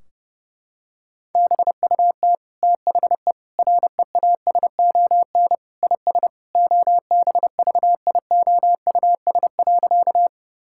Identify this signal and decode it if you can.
Morse code, heard as a keyed tone, about 32 wpm, 700 Hz. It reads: BUT THE REASON IS OBVIOUS.
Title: Sentences from Top 1600 Words 30wpm